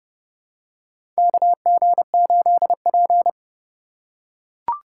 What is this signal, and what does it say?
Morse code, heard as a keyed tone, about 30 wpm, 700 Hz. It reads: KG8P
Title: USA Call Signs - 4 Characters - 4x Character Spacing - 30wpm